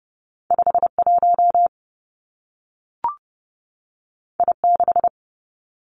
Morse code, heard as a keyed tone, about 30 wpm, 700 Hz. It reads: 5 1 I 6
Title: Two Challenging Letter-Number Combinations 30wpm